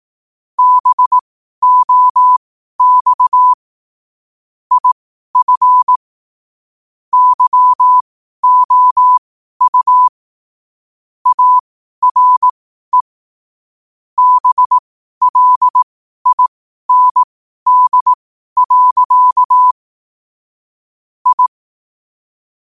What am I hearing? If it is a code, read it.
Morse code, heard as a keyed tone, about 18 wpm, spaced as for 12 wpm, 1000 Hz. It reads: BOX IF YOU ARE BLIND. I